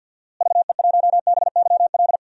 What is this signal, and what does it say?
Morse code, heard as a keyed tone, about 47 wpm, 700 Hz. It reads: VE1BYL
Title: DXCC-100 Call Signs - 6 Characters - Repeated 3x - 50wpm